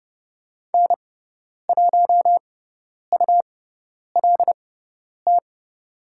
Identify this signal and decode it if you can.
Morse code, heard as a keyed tone, about 30 wpm, spaced as for 10 wpm, 700 Hz. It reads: N1ULT
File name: Lesson 18g - Letter m - Call Signs 30wpm